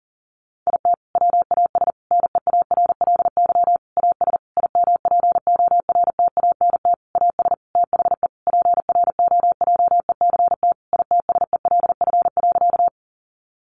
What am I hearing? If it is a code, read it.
Morse code, heard as a keyed tone, about 40 wpm, 700 Hz. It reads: IT WAS NEARLY AS IMPORTANT AS THE PROJECT ITSELF.
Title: Sentences from Top 1400 Words 40wpm